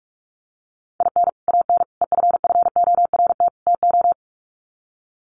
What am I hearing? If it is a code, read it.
Morse code, heard as a keyed tone, about 44 wpm, 700 Hz. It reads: IN AN EFFORT TO
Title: Sets of 4 Words 45wpm